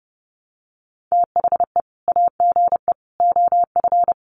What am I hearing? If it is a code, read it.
Morse code, heard as a keyed tone, about 30 wpm, 700 Hz. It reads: THE AGE OF
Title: Sets of 3 Words 30wpm